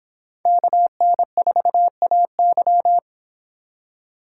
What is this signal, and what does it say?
Morse code, heard as a keyed tone, about 26 wpm, 700 Hz. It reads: KN4AY